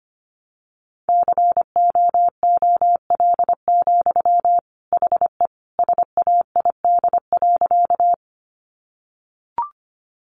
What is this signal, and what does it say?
Morse code, heard as a keyed tone, about 25 wpm, 700 Hz. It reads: COOL, HE SAID.
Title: Sentences from Top 1200 Words 25wpm